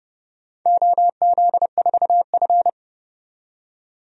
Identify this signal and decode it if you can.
Morse code, heard as a keyed tone, about 31 wpm, 700 Hz. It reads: OZ4F
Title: DXCC-100 Call Signs - 4 Characters - 30wpm